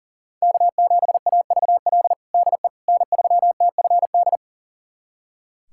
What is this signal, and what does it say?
Morse code, heard as a keyed tone, about 40 wpm, 700 Hz. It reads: K7AVL DE N3TFD
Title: USA Callsign DE Callsign - 5 Characters - 40wpm